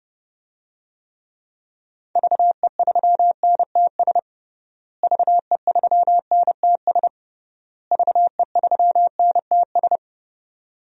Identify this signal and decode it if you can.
Morse code, heard as a keyed tone, about 32 wpm, 700 Hz. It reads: VE3NTS VE3NTS VE3NTS
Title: DXCC-100 Call Signs - 6 Characters - Repeated 3x - 30wpm